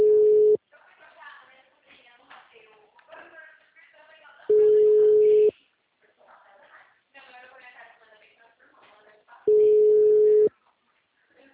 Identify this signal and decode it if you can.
Ringback tone (Europe, 425 Hz), with speech.